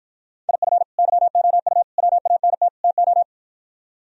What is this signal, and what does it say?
Morse code, heard as a keyed tone, about 53 wpm, 700 Hz. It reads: IF YOU WANT TO